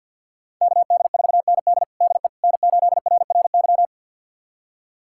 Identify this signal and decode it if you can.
Morse code, heard as a keyed tone, about 45 wpm, 700 Hz. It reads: KD4ND DE N8RRY
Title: USA Callsign DE Callsign - 5 Characters - 50wpm